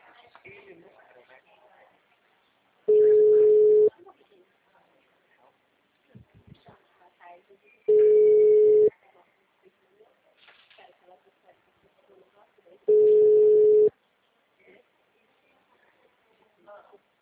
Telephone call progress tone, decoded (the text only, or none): ringback tone